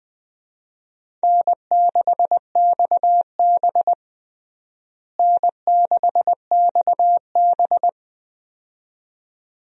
Morse code, heard as a keyed tone, about 20 wpm, 700 Hz. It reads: N6XB N6XB
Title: USA Call Signs - 4 Characters - Repeated 2x - 20wpm